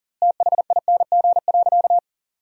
Morse code, heard as a keyed tone, about 40 wpm, 700 Hz. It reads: THING.
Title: 4x Word-Spacing - Sentences from Top 400 Words - 40wpm